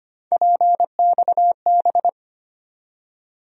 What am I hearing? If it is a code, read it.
Morse code, heard as a keyed tone, about 25 wpm, 700 Hz. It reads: PXB
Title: USA Call Sign Postfixes - 3 Letters 25wpm